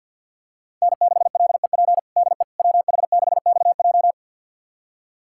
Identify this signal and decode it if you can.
Morse code, heard as a keyed tone, about 48 wpm, 700 Hz. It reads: N6LEP DE WH6XJ
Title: USA Callsign DE Callsign - 5 Characters - 50wpm